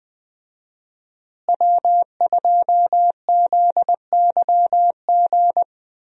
Morse code, heard as a keyed tone, about 20 wpm, 700 Hz. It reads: W2ZYG
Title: USA Call Signs - 5 Characters - Repeated 2x - 20wpm